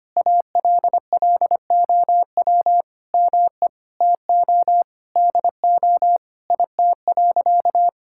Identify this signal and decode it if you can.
Morse code, heard as a keyed tone, about 25 wpm, 700 Hz. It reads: ALLOW ME TO DO IT.
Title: Sentences from Top 1000 Words - Review Part 5 - 25wpm